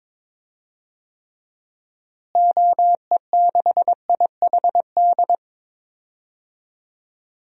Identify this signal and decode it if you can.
Morse code, heard as a keyed tone, about 22 wpm, 700 Hz. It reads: OE6IHD
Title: DXCC-100 Call Signs - 6 Characters - Repeated 2x - 22wpm